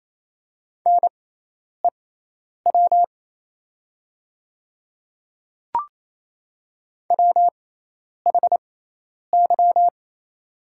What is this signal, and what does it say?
Morse code, heard as a keyed tone, about 28 wpm, spaced as for 10 wpm, 700 Hz. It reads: NEW WHY
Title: 6x Character Spacing - Top 300 Words - 28wpm